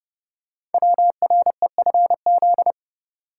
Morse code, heard as a keyed tone, about 30 wpm, 700 Hz. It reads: WREFZ
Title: Code Groups - 5 Letter:Numbers - 30wpm